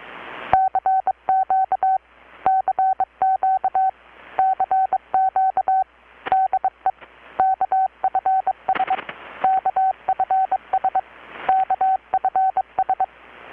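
Morse code, heard as a keyed tone, about 23 wpm, 760 Hz. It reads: CQ CQ CQ DE KFS KFS KFS
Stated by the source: Morse from KPH's Night of Nights XX maritime radiotelegraphy event, received in Chiapas, Mexico about 3,800 km away on a Kenwood R-600 with a grounded 10 m long-wire antenna.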